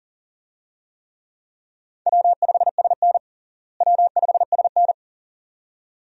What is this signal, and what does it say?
Morse code, heard as a keyed tone, about 40 wpm, 700 Hz. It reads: W5SN W5SN
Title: USA Call Signs - 4 Characters - Repeated 2x - 40wpm